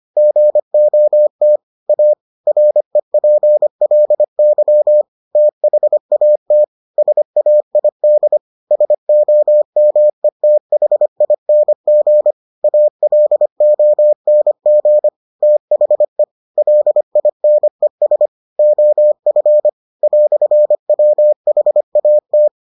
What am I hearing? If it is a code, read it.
Morse code, heard as a keyed tone, about 25 wpm, 600 Hz. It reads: GOT A REPLY THAT SAID SOMETHING ALONG THE LINES OF "WHAT